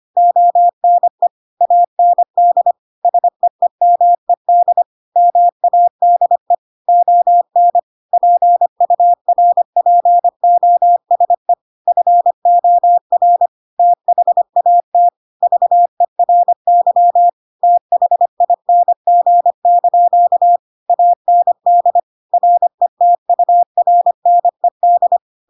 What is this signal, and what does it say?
Morse code, heard as a keyed tone, about 25 wpm, 700 Hz. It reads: ONE AND SEEMED MADE ON PURPOSE FOR THAT VERY THING) AND RETURNED
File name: Pinocchio / Pinocchio ch3 25 25 ik1hge